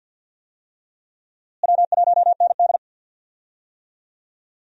Morse code, heard as a keyed tone, about 46 wpm, 700 Hz. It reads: W1ND
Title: USA Call Signs - 4 Characters - Repeated 2x - 50wpm